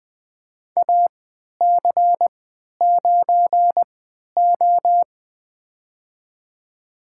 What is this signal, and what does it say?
Morse code, heard as a keyed tone, about 20 wpm, spaced as for 11 wpm, 700 Hz. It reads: AC9O
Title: USA Call Signs - 4 Characters - 3x Character Spacing - 20wpm